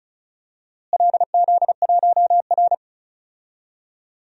Morse code, heard as a keyed tone, about 36 wpm, 700 Hz. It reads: LZ1R
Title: DXCC-100 Call Signs - 4 Characters - Repeated 2x - 35wpm